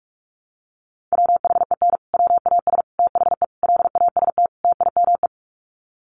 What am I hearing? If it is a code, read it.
Morse code, heard as a keyed tone, about 46 wpm, 700 Hz. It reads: WHEN WAS THE LAST TIME
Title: Sets of 5 Words Encore 45wpm